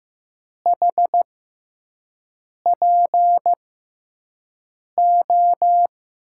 Morse code, heard as a keyed tone, about 15 wpm, spaced as for 5 wpm, 700 Hz. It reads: HPO
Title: USA Call Signs - 6x Character Spacing - 15wpm